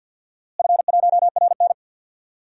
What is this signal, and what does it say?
Morse code, heard as a keyed tone, about 48 wpm, 700 Hz. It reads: F1RN
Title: DXCC-100 Call Signs - 4 Characters - Repeated 2x - 50wpm